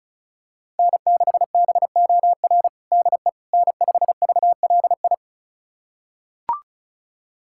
Morse code, heard as a keyed tone, about 34 wpm, 700 Hz. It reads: N6BOR DE N5VLI
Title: USA Callsign DE Callsign - 5 Characters - 35wpm